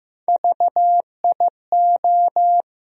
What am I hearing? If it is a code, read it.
Morse code, heard as a keyed tone, about 15 wpm, 700 Hz. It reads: VIO
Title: USA Call Sign Postfixes - 3 Letters 15wpm